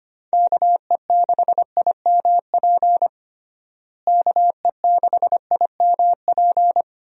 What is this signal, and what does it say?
Morse code, heard as a keyed tone, about 25 wpm, 700 Hz. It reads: KE6IMP KE6IMP
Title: USA Call Signs - 6 Characters - Repeated 3x - 25wpm